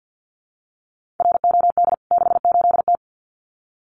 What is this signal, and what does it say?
Morse code, heard as a keyed tone, about 50 wpm, 700 Hz. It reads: ROD 68T
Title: CWT Exchange 50wpm